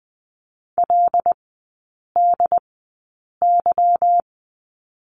Morse code, heard as a keyed tone, about 20 wpm, spaced as for 8 wpm, 700 Hz. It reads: LDY
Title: 172 Lesson 28e - Number 9 - Three Characters 20wpm